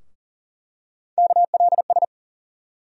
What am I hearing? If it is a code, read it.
Morse code, heard as a keyed tone, about 40 wpm, 700 Hz. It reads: KLS
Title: USA Call Sign Postfixes - 3 Letters 40wpm